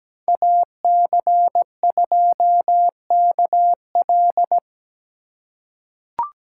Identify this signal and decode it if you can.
Morse code, heard as a keyed tone, about 17 wpm, 700 Hz. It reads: AC2KL
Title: USA Call Signs - 5 Characters - 4x Character Spacing - 17wpm